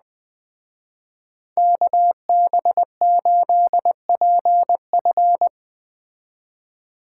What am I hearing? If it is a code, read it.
Morse code, heard as a keyed tone, about 21 wpm, 700 Hz. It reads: KB8PF
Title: Speed Racing - USA 2x2 Call Signs 17wpm